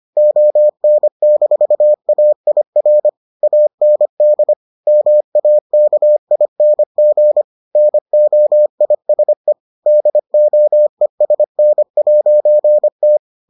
Morse code, heard as a keyed tone, about 25 wpm, 600 Hz. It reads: ON-AIR AND MAKING NOISE DOESN'T